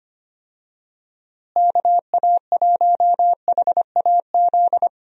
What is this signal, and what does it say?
Morse code, heard as a keyed tone, about 25 wpm, 700 Hz. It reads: KA1HAZ
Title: USA Call Signs - 6 Characters - Repeated 2x - 25wpm